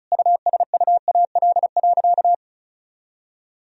Morse code, heard as a keyed tone, about 36 wpm, 700 Hz. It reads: USUAL.
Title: Sentences from Top 1000 Words - Review Part 2 - 35wpm